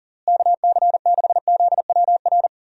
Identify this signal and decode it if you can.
Morse code, heard as a keyed tone, about 38 wpm, 700 Hz. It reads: KC6ZWR
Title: USA Call Signs - 6 Characters - Repeated 3x - 40wpm